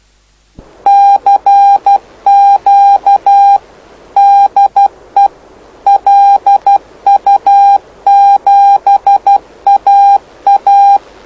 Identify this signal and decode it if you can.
Morse code, heard as a keyed tone, about 12 wpm, 800 Hz. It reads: CQ DE LU7AA
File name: lu4aao / lu7aa 09dic2011 exp planeador vuelo1